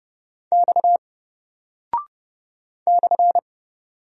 Morse code, heard as a keyed tone, about 30 wpm, 700 Hz. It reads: X /
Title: Single Character (Letter:Number:Punctuation:Prosign) 30wpm